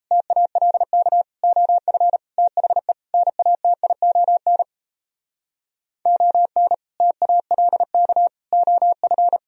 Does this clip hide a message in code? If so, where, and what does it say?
Yes: Morse code from 0.1 to 9.4 s, heard as a keyed tone, about 36 wpm, 700 Hz. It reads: TALK OF THE NATION ON TALK OF